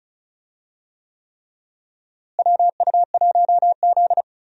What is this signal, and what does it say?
Morse code, heard as a keyed tone, about 36 wpm, 700 Hz. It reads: WU1Z
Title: USA Call Signs - 4 Characters - Repeated 3x - 35wpm